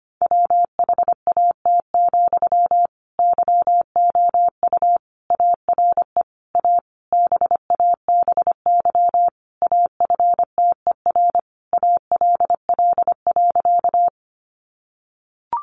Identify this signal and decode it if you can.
Morse code, heard as a keyed tone, about 25 wpm, 700 Hz. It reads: WHAT, YOU ARE A BABY AFTER ALL.
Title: Sentences from Top 1300 Words 25wpm